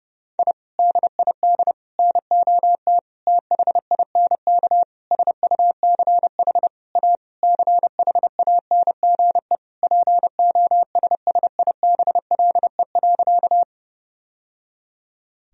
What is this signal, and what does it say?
Morse code, heard as a keyed tone, about 30 wpm, 700 Hz. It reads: I DID NOT THINK SUCH A CHANGE POSSIBLE.
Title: Sentences from Top 1000 Words - Review Part 1 - 30wpm